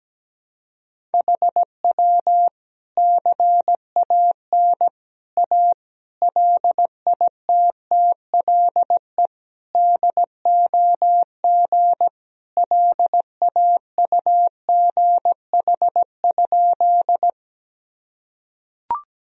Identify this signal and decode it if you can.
Morse code, heard as a keyed tone, about 17 wpm, 700 Hz. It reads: HW CAN A LITTLE DOG LAUGH?
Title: Sentences from Top 500 Words and Abbreviations 17wpm